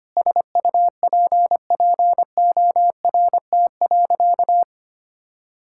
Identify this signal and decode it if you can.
Morse code, heard as a keyed tone, about 25 wpm, 700 Hz. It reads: SUPPORT.
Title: Sentences from Top 1000 Words - Review Part 5 - 25wpm